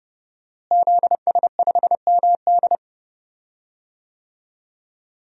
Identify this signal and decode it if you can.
Morse code, heard as a keyed tone, about 32 wpm, 700 Hz. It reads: ZS5MD
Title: DXCC-100 Call Signs - 5 Characters - Repeated 2x - 30wpm